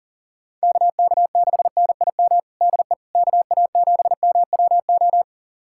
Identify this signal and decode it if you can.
Morse code, heard as a keyed tone, about 40 wpm, 700 Hz. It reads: KK6NIM DE KA7MWO
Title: USA Callsign DE Callsign - 6 Characters - 40wpm